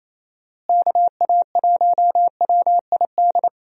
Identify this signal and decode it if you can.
Morse code, heard as a keyed tone, about 28 wpm, 700 Hz. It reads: KA1WID
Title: USA Call Signs - 6 Characters - Repeated 3x - 28wpm